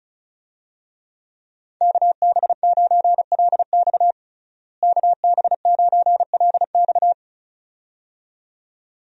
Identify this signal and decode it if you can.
Morse code, heard as a keyed tone, about 36 wpm, 700 Hz. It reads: KB9LX KB9LX
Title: USA Call Signs - 5 Characters - Repeated 2x - 35wpm